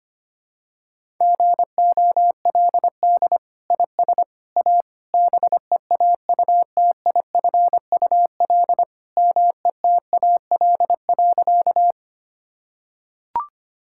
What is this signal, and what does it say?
Morse code, heard as a keyed tone, about 25 wpm, 700 Hz. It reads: GOLD IS A BEAUTIFUL METAL.
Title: Sentences from Top 1000 Words - Review Part 6 - 25wpm